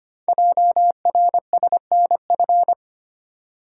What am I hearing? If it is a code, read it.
Morse code, heard as a keyed tone, about 25 wpm, 700 Hz. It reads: JRSNF
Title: Code Groups - 5 Letters - 25wpm